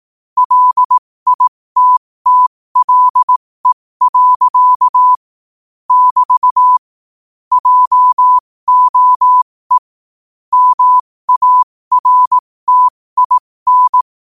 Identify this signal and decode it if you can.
Morse code, heard as a keyed tone, about 18 wpm, spaced as for 15 wpm, 1000 Hz. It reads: LITTLE. <BT> JOE MARTIN
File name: 15wpm